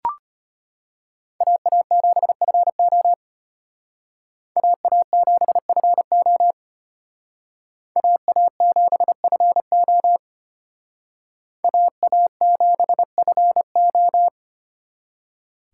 Morse code, heard as a keyed tone, about 33 wpm, 700 Hz. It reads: AA7FO AA7FO AA7FO AA7FO